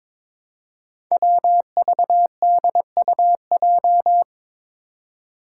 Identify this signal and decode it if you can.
Morse code, heard as a keyed tone, about 22 wpm, 700 Hz. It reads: WVDUJ
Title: Five Challenging Letter-Number Combinations 22wpm